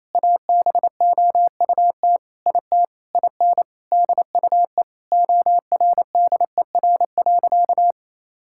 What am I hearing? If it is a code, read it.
Morse code, heard as a keyed tone, about 28 wpm, 700 Hz. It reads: ABOUT IT IN DUE ORDER.
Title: Sentences from Top 1000 Words - Review Part 6 - 28wpm